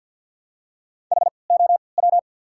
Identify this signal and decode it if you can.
Morse code, heard as a keyed tone, about 53 wpm, spaced as for 29 wpm, 690 Hz. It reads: HOW
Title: 3x Character Spacing - Top 100 Words - 50wpm